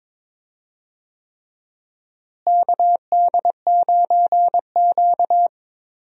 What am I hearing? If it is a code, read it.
Morse code, heard as a keyed tone, about 22 wpm, 700 Hz. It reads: KD9Q
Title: USA Call Signs - 4 Characters - Repeated 3x - 22wpm